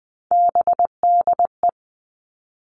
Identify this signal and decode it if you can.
Morse code, heard as a keyed tone, about 19 wpm, 700 Hz. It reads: BDE
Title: Three Challenging Letter-Number Combinations 20wpm